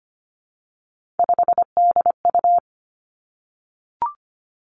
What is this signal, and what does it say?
Morse code, heard as a keyed tone, about 25 wpm, 700 Hz. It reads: 5DU
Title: Three Challenging Letter-Number Combinations 25wpm